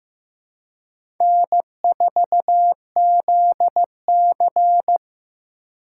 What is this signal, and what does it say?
Morse code, heard as a keyed tone, about 15 wpm, 700 Hz. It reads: N4ZC
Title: USA Call Signs - 4 Characters - 3x Character Spacing - 15wpm